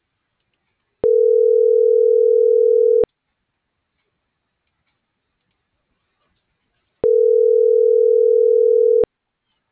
Ringback tone (North America), 440 + 480 Hz.